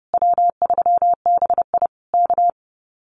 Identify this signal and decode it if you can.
Morse code, heard as a keyed tone, about 30 wpm, 700 Hz. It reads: W3BI K
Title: Common QSO Phrases Encore 30wpm